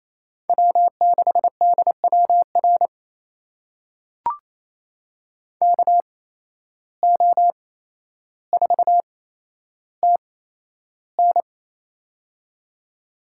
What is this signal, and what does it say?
Morse code, heard as a keyed tone, about 28 wpm, spaced as for 12 wpm, 700 Hz. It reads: W6DWR KO4TN
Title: USA Call Signs - 5 Characters - 8x Character Spacing - 28wpm